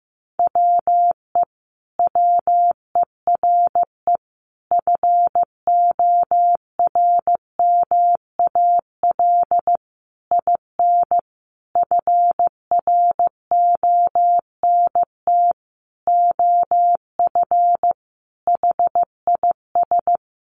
Morse code, heard as a keyed tone, about 14 wpm, 700 Hz. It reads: WE WERE FORMAL IN FRONT OF HIS
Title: Sentences from Top 1300 Words 15wpm